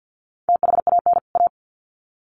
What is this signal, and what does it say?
Morse code, heard as a keyed tone, about 50 wpm, 700 Hz. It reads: THAN A
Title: Sets of 2 Words 50wpm